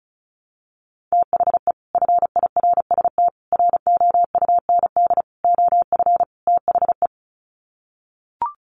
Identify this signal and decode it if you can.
Morse code, heard as a keyed tone, about 34 wpm, 700 Hz. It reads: THE FIRST ROUND OF THE
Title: Sets of 5 Words Encore 35wpm